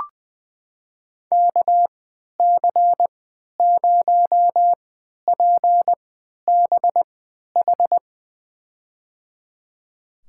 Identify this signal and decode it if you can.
Morse code, heard as a keyed tone, about 20 wpm, spaced as for 11 wpm, 700 Hz. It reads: KC0PBH
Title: USA Call Signs - 6 Characters - 3x Character Spacing - 20wpm